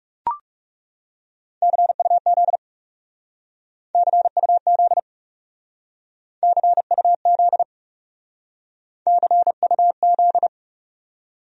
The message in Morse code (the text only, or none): CUZ CUZ CUZ CUZ